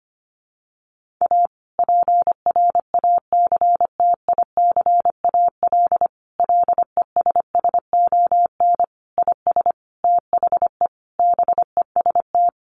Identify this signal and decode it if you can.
Morse code, heard as a keyed tone, about 25 wpm, 700 Hz. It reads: A PRACTICAL LESSON IS THE BEST
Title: Sentences from Top 1400 Words 25wpm